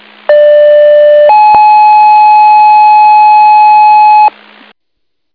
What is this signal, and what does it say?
Two-tone page, heard as a one-second tone, then a three-second tone, about 590 then 850 Hz, with noise. It is heard from 0.3 to 4.3 s.